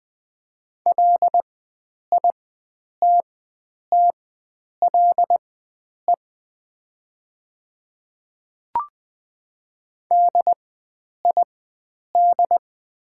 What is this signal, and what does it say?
Morse code, heard as a keyed tone, about 20 wpm, spaced as for 9 wpm, 700 Hz. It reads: LITTLE DID
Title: 4x Character Spacing - Top 500 Words - 20wpm